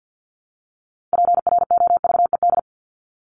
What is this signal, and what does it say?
Morse code, heard as a keyed tone, about 50 wpm, 700 Hz. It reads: PROVED
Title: ED Suffix - Top 100 Words with Base Words 50wpm